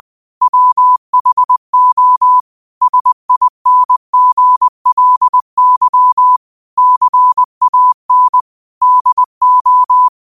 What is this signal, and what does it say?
Morse code, heard as a keyed tone, about 20 wpm, 1000 Hz. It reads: WHO SINGLY CAN DO
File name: qotdfiles / 20wpm